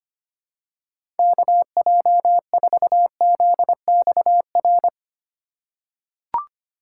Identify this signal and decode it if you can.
Morse code, heard as a keyed tone, about 25 wpm, 700 Hz. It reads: KJ4ZXR
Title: USA Call Signs - 6 Characters - 4x Character Spacing - 25wpm